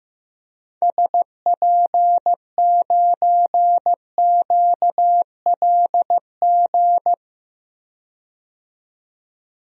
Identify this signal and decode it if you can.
Morse code, heard as a keyed tone, about 16 wpm, 700 Hz. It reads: SP9QLG
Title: DXCC-100 Call Signs - 6 Characters - 15wpm